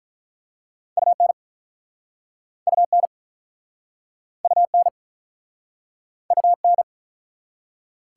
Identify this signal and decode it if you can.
Morse code, heard as a keyed tone, about 44 wpm, 700 Hz. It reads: UN UN UN UN